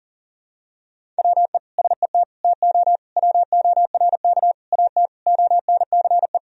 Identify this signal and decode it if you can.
Morse code, heard as a keyed tone, about 40 wpm, 700 Hz. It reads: WE SET TO WORK AT ONCE
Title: Sentences from Top 1000 Words - Review Part 1 - 40wpm